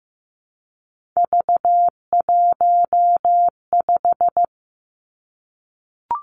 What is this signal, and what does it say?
Morse code, heard as a keyed tone, about 15 wpm, 700 Hz. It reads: V15
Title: Three Challenging Letter-Number Combinations 15wpm